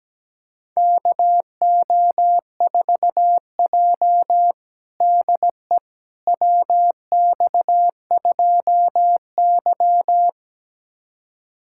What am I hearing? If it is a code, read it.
Morse code, heard as a keyed tone, about 17 wpm, 700 Hz. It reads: KO4J DE WX2Y
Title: USA Callsign DE Callsign - 4 Characters - 17wpm